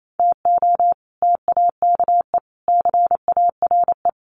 Morse code, heard as a keyed tone, about 28 wpm, 700 Hz. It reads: TO TAKE CARE
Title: Sets of 5 Words Encore 28wpm